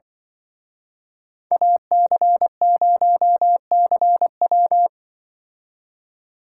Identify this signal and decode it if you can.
Morse code, heard as a keyed tone, about 24 wpm, 700 Hz. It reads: AC0CW
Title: Speed Racing - USA 2x2 Call Signs 20wpm